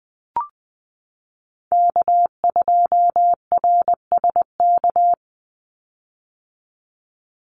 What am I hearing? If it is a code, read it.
Morse code, heard as a keyed tone, about 20 wpm, 700 Hz. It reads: K2RSK
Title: Call Signs - 20wpm